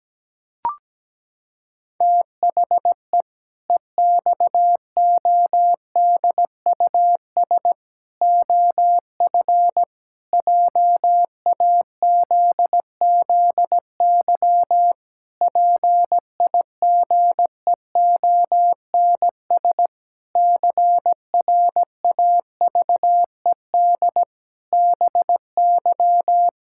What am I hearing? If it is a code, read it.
Morse code, heard as a keyed tone, about 17 wpm, 700 Hz. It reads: THE EXODUS OF JAZZY PIGEONS CRAVED BY
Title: Pangram Challenge v2 - 17wpm